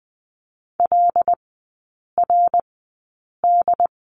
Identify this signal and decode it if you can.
Morse code, heard as a keyed tone, about 20 wpm, spaced as for 8 wpm, 700 Hz. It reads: LRD
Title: 081 Lesson 14e - Number 2 - Three Characters 20wpm